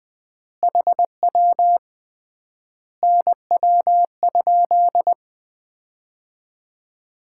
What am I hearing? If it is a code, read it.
Morse code, heard as a keyed tone, about 20 wpm, 700 Hz. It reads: HW NW?
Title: Common QSO Phrases - 3x Word-Spacing - 20wpm